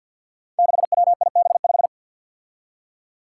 Morse code, heard as a keyed tone, about 46 wpm, 700 Hz. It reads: 6WIB5
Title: Five Challenging Letter-Number Combinations 50wpm